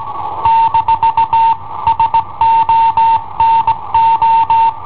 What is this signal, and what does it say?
Morse code, heard as a keyed tone, about 19 wpm, 910 Hz. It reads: - SONO
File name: i7ohp / cassetta i7ohp B